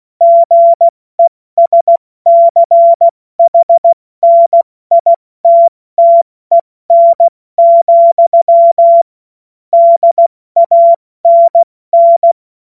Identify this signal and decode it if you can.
Morse code, heard as a keyed tone, about 16 wpm, 680 Hz. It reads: GESCHNITTEN, DANN
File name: Morsekurs / Witze / Witz 18. Juli 2023 70 F80